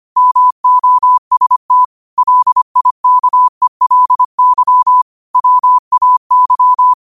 Morse code, heard as a keyed tone, about 25 wpm, 1000 Hz. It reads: MOST LIKELY WAY
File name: qotdfiles / 25wpm